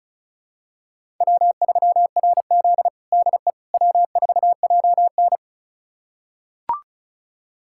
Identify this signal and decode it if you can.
Morse code, heard as a keyed tone, about 36 wpm, 700 Hz. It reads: W3RZ DE W4JN
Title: USA Callsign DE Callsign - 4 Characters - 35wpm